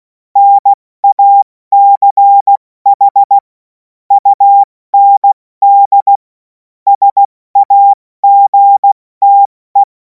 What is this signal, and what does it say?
Morse code, heard as a keyed tone, about 16 wpm, spaced as for 14 wpm, 800 Hz. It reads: NACH UND SAGTE
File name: Morsekurs / Witze / Witz 24. März 2026 70 F80